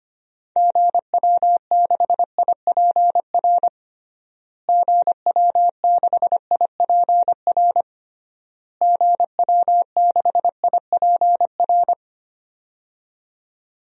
Morse code, heard as a keyed tone, about 25 wpm, 700 Hz. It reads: GW6IPR GW6IPR GW6IPR
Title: DXCC-100 Call Signs - 6 Characters - Repeated 3x - 25wpm